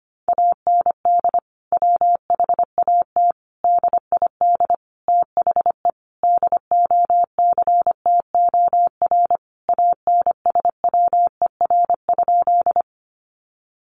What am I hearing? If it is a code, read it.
Morse code, heard as a keyed tone, about 25 wpm, 700 Hz. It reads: AND WHAT DID THE DOCTOR ANSWER?
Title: Sentences from Top 1500 Words 25wpm